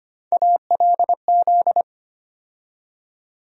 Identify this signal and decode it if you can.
Morse code, heard as a keyed tone, about 25 wpm, 700 Hz. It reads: ALZ
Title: USA Call Sign Postfixes - 3 Letters 25wpm